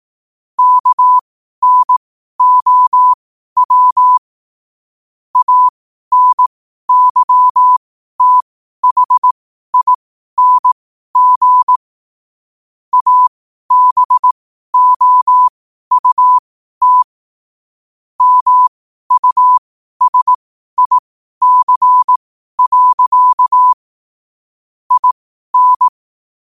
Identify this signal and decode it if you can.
Morse code, heard as a keyed tone, about 18 wpm, spaced as for 12 wpm, 1000 Hz. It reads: KNOW ANYTHING ABOUT MUSIC. IN
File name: qotdfiles / 12wpm